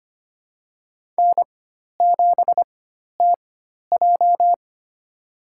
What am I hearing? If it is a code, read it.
Morse code, heard as a keyed tone, about 25 wpm, spaced as for 12 wpm, 700 Hz. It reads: N7TJ
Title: USA Call Signs - 4 Characters - 4x Character Spacing - 25wpm